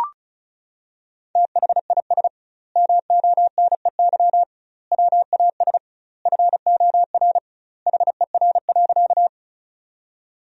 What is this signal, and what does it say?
Morse code, heard as a keyed tone, about 36 wpm, 700 Hz. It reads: THIS MONEY WAS FOR HER.
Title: 2x Word-Spacing - Sentences from Top 400 Words - 35wpm